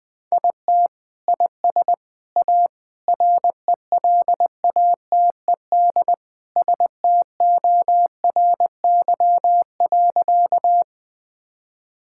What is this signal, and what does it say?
Morse code, heard as a keyed tone, about 21 wpm, 700 Hz. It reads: IT IS A RELATED STORY.
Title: Sentences from Top 1000 Words - Review Part 5 - 20wpm